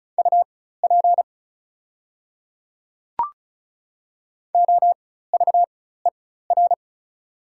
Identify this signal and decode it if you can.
Morse code, heard as a keyed tone, about 35 wpm, spaced as for 16 wpm, 700 Hz. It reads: UP OVER
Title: 4x Character Spacing - Top 100 Words - 35wpm